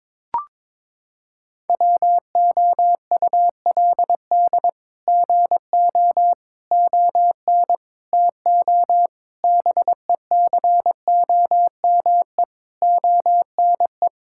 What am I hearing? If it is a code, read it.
Morse code, heard as a keyed tone, about 22 wpm, 700 Hz. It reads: WOULD GO ON TO BECOME ONE
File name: practice / 22wpm / Sets of 8 Words - 22wpm